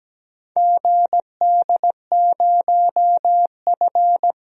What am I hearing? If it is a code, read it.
Morse code, heard as a keyed tone, about 17 wpm, 700 Hz. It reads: GD0F
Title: DXCC-100 Call Signs - 4 Characters - 17wpm